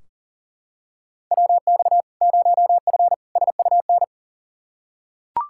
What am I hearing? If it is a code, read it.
Morse code, heard as a keyed tone, about 40 wpm, 700 Hz. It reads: WX 0F SUN
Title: Practice QSO - Exchange 2 - Info - 40wpm